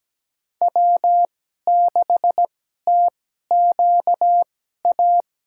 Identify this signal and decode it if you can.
Morse code, heard as a keyed tone, about 17 wpm, spaced as for 12 wpm, 700 Hz. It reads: W6TQA
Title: USA Call Signs - 5 Characters - 2x Character Spacing - 17wpm